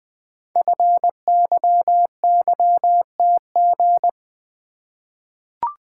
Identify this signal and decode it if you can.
Morse code, heard as a keyed tone, about 20 wpm, 700 Hz. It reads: FYYTG
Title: Code Groups - 5 Letters - 20wpm